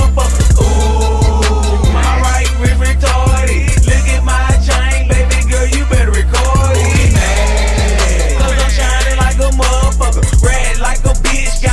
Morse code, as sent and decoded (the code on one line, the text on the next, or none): none